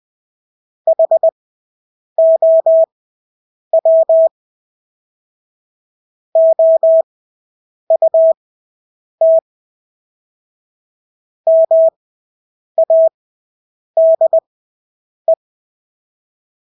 Morse code, heard as a keyed tone, about 20 wpm, spaced as for 8 wpm, 650 Hz. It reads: HOW OUT MADE
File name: Engelske ord 20/8 wpm